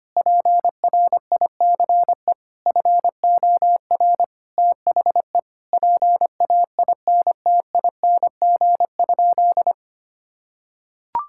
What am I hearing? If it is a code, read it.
Morse code, heard as a keyed tone, about 25 wpm, 700 Hz. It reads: PRICE FOR THE PAINTING?
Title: Sentences from Top 1000 Words - Review Part 6 - 25wpm